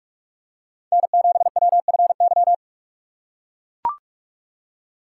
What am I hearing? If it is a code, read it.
Morse code, heard as a keyed tone, about 41 wpm, 700 Hz. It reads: N7WFY